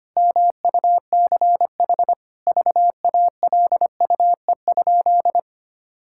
Morse code, heard as a keyed tone, about 25 wpm, 700 Hz. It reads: MUCH VALUE?
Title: Sentences from Top 1000 Words - Review Part 6 - 25wpm